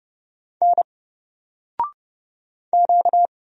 Morse code, heard as a keyed tone, about 30 wpm, 700 Hz. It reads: N Q
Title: Single Character (Letter:Number:Punctuation:Prosign) 30wpm